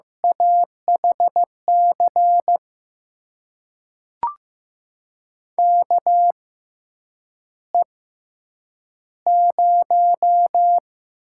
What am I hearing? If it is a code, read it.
Morse code, heard as a keyed tone, about 16 wpm, spaced as for 8 wpm, 700 Hz. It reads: AHC KE0
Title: USA Call Signs - 6 Characters - 6x Character Spacing - 15wpm